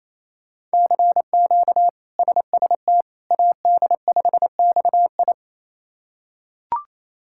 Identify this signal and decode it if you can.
Morse code, heard as a keyed tone, about 28 wpm, 700 Hz. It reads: CQ SST AD5XI
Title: K1USN SST QSO Practice - 28wpm